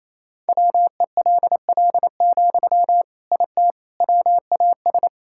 Morse code, heard as a keyed tone, about 28 wpm, 700 Hz. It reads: WELL, IT WAS
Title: Sentences from Top 1000 Words - Review Part 6 - 28wpm